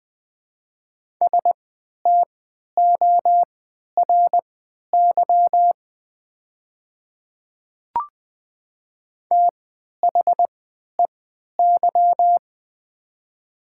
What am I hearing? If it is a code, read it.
Morse code, heard as a keyed tone, about 20 wpm, spaced as for 11 wpm, 700 Hz. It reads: STORY THEY
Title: 3x Character Spacing - Top 300 Words - 20wpm